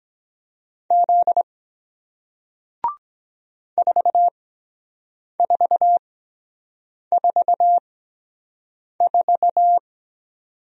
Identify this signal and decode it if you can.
Morse code, heard as a keyed tone, about 23 wpm, spaced as for 7 wpm, 700 Hz. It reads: Z 4444